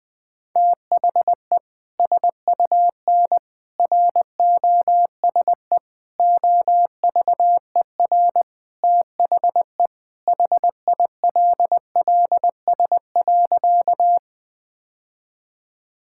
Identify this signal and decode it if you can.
Morse code, heard as a keyed tone, about 20 wpm, 700 Hz. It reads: THE SUN ROSE OVER THE HILLS.
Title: Sentences with 2-Word Phrases - 20wpm